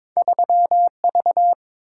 Morse code, heard as a keyed tone, about 23 wpm, 700 Hz. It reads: 3V
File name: practice / 22wpm / Two Letter-Number Combinations - 22wpm